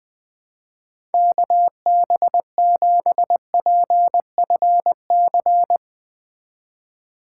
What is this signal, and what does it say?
Morse code, heard as a keyed tone, about 20 wpm, 700 Hz. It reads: KB7PFC
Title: USA Call Signs - 6 Characters - 2x Character Spacing - 20wpm